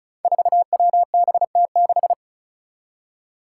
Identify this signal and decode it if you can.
Morse code, heard as a keyed tone, about 36 wpm, 700 Hz. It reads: 4WBT6
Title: Five Challenging Letter-Number Combinations 35wpm